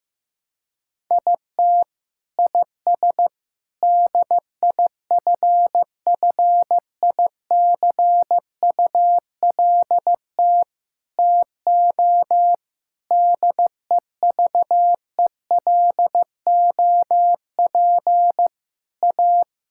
Morse code, heard as a keyed tone, about 15 wpm, 700 Hz. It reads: IT IS DIFFICULT TO DEVELOP A
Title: Sentences from Top 1000 Words - Review Part 5 - 15wpm